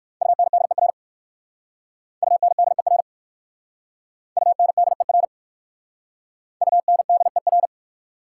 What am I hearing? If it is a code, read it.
Morse code, heard as a keyed tone, about 56 wpm, 690 Hz. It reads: UNDER UNDER UNDER UNDER
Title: Speed-Racing Prefixes and Suffixes 45wpm